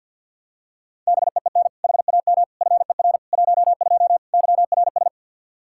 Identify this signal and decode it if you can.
Morse code, heard as a keyed tone, about 48 wpm, 700 Hz. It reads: BEEN HAM FER 12 YRS
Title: Practice QSO - Exchange 3 - More Info - 50wpm